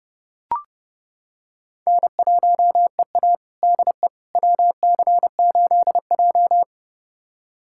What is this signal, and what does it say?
Morse code, heard as a keyed tone, about 29 wpm, 700 Hz. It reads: N1EA DE WC8J
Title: USA Callsign DE Callsign - 4 Characters - 30wpm